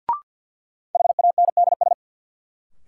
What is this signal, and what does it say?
Morse code, heard as a keyed tone, about 46 wpm, 700 Hz. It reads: HANDS
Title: Sets of 4 Words - New Word Drill - 50wpm